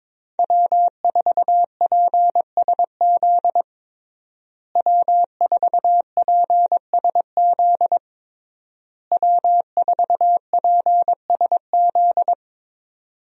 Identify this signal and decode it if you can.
Morse code, heard as a keyed tone, about 22 wpm, 700 Hz. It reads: W4PSZ W4PSZ W4PSZ
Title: USA Call Signs - 5 Characters - Repeated 3x - 22wpm